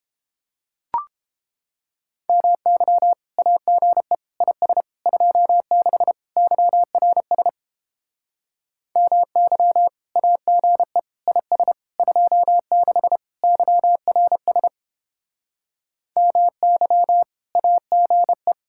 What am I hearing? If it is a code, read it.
Morse code, heard as a keyed tone, about 31 wpm, 700 Hz. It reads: MY AGE IS 26 YRS MY AGE IS 26 YRS MY AGE